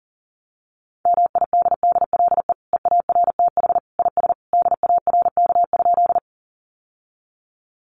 Morse code, heard as a keyed tone, about 40 wpm, 700 Hz. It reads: MIDDLE EARTH IS DARK?
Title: Sentences from Top 900 Words 40wpm